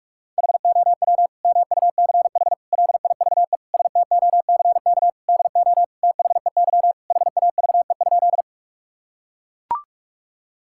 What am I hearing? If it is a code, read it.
Morse code, heard as a keyed tone, about 42 wpm, 700 Hz. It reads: HOW MUCH LIVE STOCK DO THEY HAVE?
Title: Sentences from Top 1000 Words - Review Part 5 - 45wpm